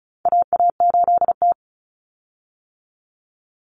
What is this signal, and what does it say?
Morse code, heard as a keyed tone, about 34 wpm, 700 Hz. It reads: AA8T
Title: Common Cut Numbers - 4-Digit 35wpm